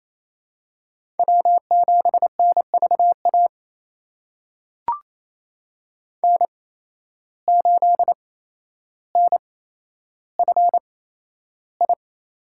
Morse code, heard as a keyed tone, about 28 wpm, spaced as for 12 wpm, 700 Hz. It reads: W7NVA N8NFI
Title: USA Call Signs - 5 Characters - 8x Character Spacing - 28wpm